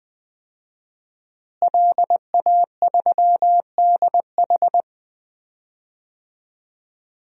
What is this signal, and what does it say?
Morse code, heard as a keyed tone, about 20 wpm, 700 Hz. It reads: LA3DH
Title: DXCC-100 Call Signs - 5 Characters - Repeated 2x - 20wpm